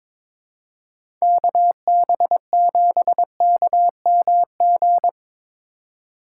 Morse code, heard as a keyed tone, about 22 wpm, 700 Hz. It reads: KB7KMG
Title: USA Call Signs - 6 Characters - 3x Character Spacing - 22wpm